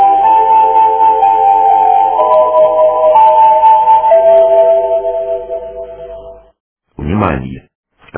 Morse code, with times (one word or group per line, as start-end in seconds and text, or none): none